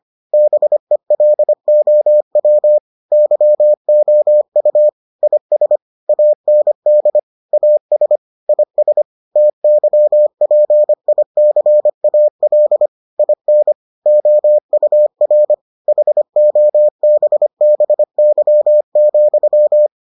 Morse code, heard as a keyed tone, about 25 wpm, 600 Hz. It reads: BELOW YOU IS AND AS IS TYPICAL IN OUR HOBBY,